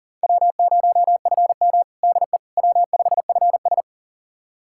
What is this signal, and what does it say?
Morse code, heard as a keyed tone, about 40 wpm, 700 Hz. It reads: W0FM DE W5FS
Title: USA Callsign DE Callsign - 4 Characters - 40wpm